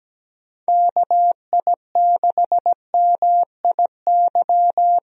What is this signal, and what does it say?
Morse code, heard as a keyed tone, about 17 wpm, 700 Hz. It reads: KI6MIY
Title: USA Call Signs - 6 Characters - Repeated 2x - 17wpm